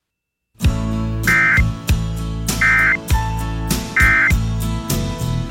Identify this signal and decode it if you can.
Emergency Alert System (SAME) tones, heard as harsh buzzing bursts. It is heard from 1.2 to 4.4 s.